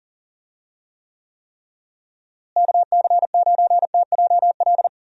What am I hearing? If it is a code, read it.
Morse code, heard as a keyed tone, about 37 wpm, 700 Hz. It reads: KC9TJL
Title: USA Call Signs - 6 Characters - Repeated 3x - 40wpm